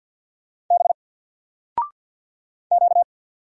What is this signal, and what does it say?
Morse code, heard as a keyed tone, about 48 wpm, 700 Hz. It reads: B Q
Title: Single Character (Letter:Number:Punctuation:Prosign) Alt 50wpm